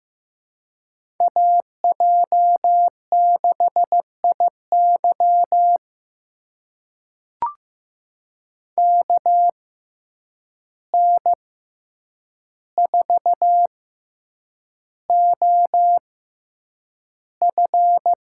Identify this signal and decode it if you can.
Morse code, heard as a keyed tone, about 15 wpm, spaced as for 8 wpm, 700 Hz. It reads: AJ6IY KN4OF
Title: USA Call Signs - 5 Characters - 6x Character Spacing - 15wpm